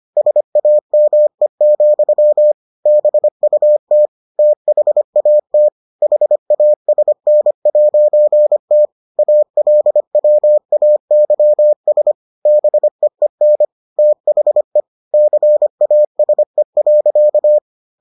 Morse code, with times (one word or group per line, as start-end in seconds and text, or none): 0.2-2.5 SAME,
2.9-4.1 BUT
4.4-5.7 THAT
6.0-8.9 HASN'T
9.2-12.1 ALWAYS
12.5-13.6 BEEN
14.0-14.8 THE
15.1-17.6 CASE.